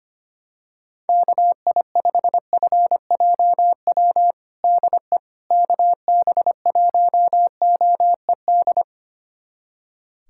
Morse code, heard as a keyed tone, about 25 wpm, 700 Hz. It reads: KI5FJW DE KB1OED
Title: USA Callsign DE Callsign - 6 Characters - 25wpm